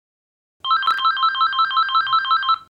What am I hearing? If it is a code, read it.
There is no code.